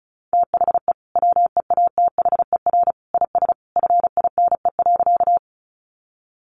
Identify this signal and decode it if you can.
Morse code, heard as a keyed tone, about 34 wpm, 700 Hz. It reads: THE WEATHER IS FINE.
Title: Sentences from Top 800 Words 35wpm